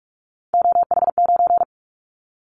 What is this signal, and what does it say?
Morse code, heard as a keyed tone, about 44 wpm, 700 Hz. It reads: OH9
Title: Callsign Prefixes for Top 50 DXCC - 45wpm